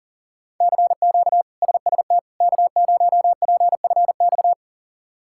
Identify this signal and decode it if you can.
Morse code, heard as a keyed tone, about 40 wpm, 700 Hz. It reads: CQ SST K0PFX
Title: K1USN SST QSO Practice - 40wpm